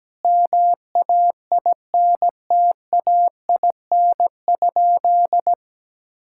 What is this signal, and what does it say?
Morse code, heard as a keyed tone, about 17 wpm, 700 Hz. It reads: MAINTAIN?
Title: Sentences from Top 1000 Words - Review Part 5 - 17wpm